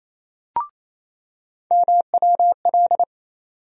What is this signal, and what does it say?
Morse code, heard as a keyed tone, about 28 wpm, 700 Hz. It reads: MWL
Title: USA Call Sign Postfixes - 3 Letters 28wpm